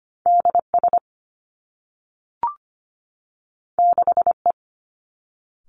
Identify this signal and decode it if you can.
Morse code, heard as a keyed tone, about 25 wpm, 700 Hz. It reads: DS 6E
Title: Two Challenging Letter-Number Combinations 25wpm